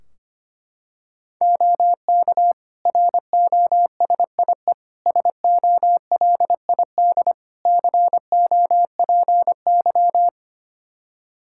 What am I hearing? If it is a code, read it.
Morse code, heard as a keyed tone, about 25 wpm, 700 Hz. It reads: OK ROSIE SOLID COPY
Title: Practice QSO - Exchange 3 - More Info - 25wpm